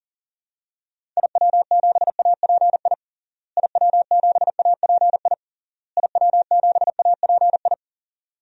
Morse code, heard as a keyed tone, about 37 wpm, 700 Hz. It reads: IW7API IW7API IW7API
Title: DXCC-100 Call Signs - 6 Characters - Repeated 3x - 40wpm